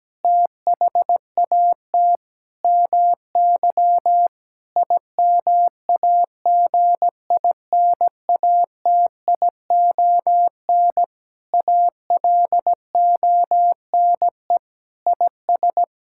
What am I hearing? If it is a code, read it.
Morse code, heard as a keyed tone, about 17 wpm, 700 Hz. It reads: THAT MY IMAGINATION ALONE IS
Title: Sentences from Top 1600 Words 17wpm